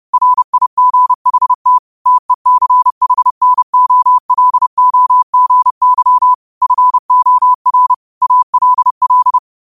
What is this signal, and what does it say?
Morse code, heard as a keyed tone, about 30 wpm, 1000 Hz. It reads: RIGHT TECHNOLOGY FOR ALL